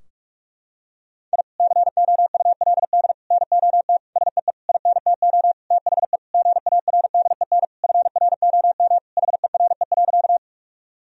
Morse code, heard as a keyed tone, about 41 wpm, 700 Hz. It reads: I COULD NOT SEE INTO THE GARDEN FROM HERE.
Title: Sentences from Top 1000 Words - Review Part 4 - 45wpm